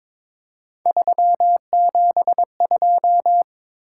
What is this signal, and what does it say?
Morse code, heard as a keyed tone, about 22 wpm, 700 Hz. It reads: 372
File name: practice / 22wpm / Three Letter-Number Combinations - 22wpm